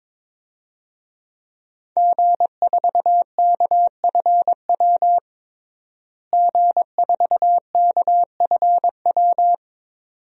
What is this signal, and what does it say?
Morse code, heard as a keyed tone, about 22 wpm, 700 Hz. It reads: G4KFW G4KFW
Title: DXCC-100 Call Signs - 5 Characters - Repeated 2x - 22wpm